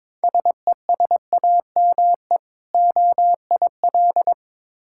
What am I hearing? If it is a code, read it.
Morse code, heard as a keyed tone, about 22 wpm, 690 Hz. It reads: SESAME OIL
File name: Difficult Words with Hints - 22wpm